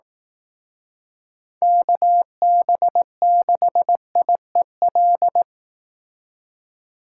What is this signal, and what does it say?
Morse code, heard as a keyed tone, about 19 wpm, 700 Hz. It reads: KB6IEL
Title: Speed Racing - USA 2x3 Call Signs 15wpm